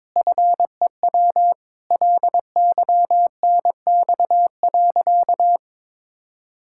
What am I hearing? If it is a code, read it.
Morse code, heard as a keyed tone, about 22 wpm, 700 Hz. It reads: FEW LYNX.
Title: Pangram Challenge v2 - 22wpm